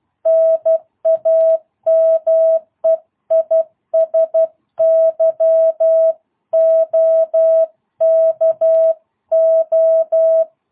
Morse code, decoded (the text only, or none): NAME IS YOKO